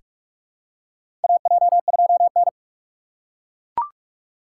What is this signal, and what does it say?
Morse code, heard as a keyed tone, about 42 wpm, 700 Hz. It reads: AJ2N
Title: USA Call Signs - 2x Character Spacing - 45wpm